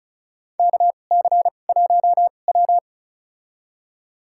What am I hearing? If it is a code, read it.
Morse code, heard as a keyed tone, about 36 wpm, spaced as for 25 wpm, 700 Hz. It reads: KC1W
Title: USA Call Signs - 2x Character Spacing - 35wpm